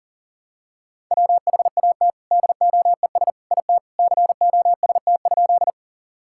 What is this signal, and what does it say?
Morse code, heard as a keyed tone, about 40 wpm, 700 Hz. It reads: WHAT DOES IT COST?
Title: Sentences from Top 1000 Words - Review Part 5 - 40wpm